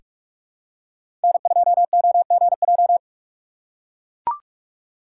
Morse code, heard as a keyed tone, about 42 wpm, 700 Hz. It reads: N2OGJ
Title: USA Call Signs - 3x Character Spacing - 45wpm